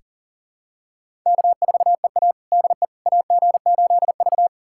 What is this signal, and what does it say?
Morse code, heard as a keyed tone, about 40 wpm, 700 Hz. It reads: K4EA DE AG8V
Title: USA Callsign DE Callsign - 4 Characters - 40wpm